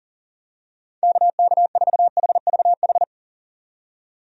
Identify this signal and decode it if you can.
Morse code, heard as a keyed tone, about 40 wpm, 700 Hz. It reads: KK4HVH
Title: USA Call Signs - 6 Characters - 40wpm